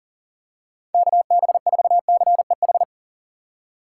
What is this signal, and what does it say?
Morse code, heard as a keyed tone, about 40 wpm, 700 Hz. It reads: KB4CEH
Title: USA Call Signs - 6 Characters - 6x Character Spacing - 40wpm